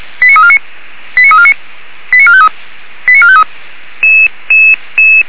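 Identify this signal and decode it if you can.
Five-tone selective call (ZVEI), heard as quick runs of five tones, tones of about 70 ms. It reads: 89349 89349 89453 89453
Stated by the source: ZVEI five-tone call 89609 received by SDR.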